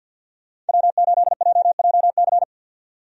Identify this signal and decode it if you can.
Morse code, heard as a keyed tone, about 47 wpm, 700 Hz. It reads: W8JJC
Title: USA Call Signs - 5 Characters - 50wpm